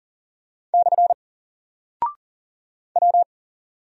Morse code, heard as a keyed tone, about 38 wpm, 700 Hz. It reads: / W
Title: Single Character (Letter:Number:Punctuation:Prosign) Alt 40wpm